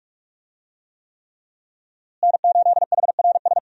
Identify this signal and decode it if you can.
Morse code, heard as a keyed tone, about 45 wpm, 700 Hz. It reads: N8HRS
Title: USA Call Signs - 5 Characters - Repeated 3x - 45wpm